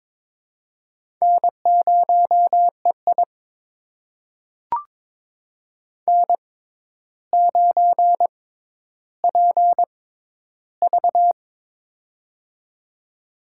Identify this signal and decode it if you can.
Morse code, heard as a keyed tone, about 22 wpm, spaced as for 11 wpm, 700 Hz. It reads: N0EI N9PV
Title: USA Call Signs - 4 Characters - 6x Character Spacing - 22wpm